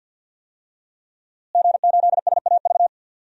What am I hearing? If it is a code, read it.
Morse code, heard as a keyed tone, about 52 wpm, 700 Hz. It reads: G8SAV
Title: DXCC-100 Call Signs - 5 Characters - Repeated 2x - 50wpm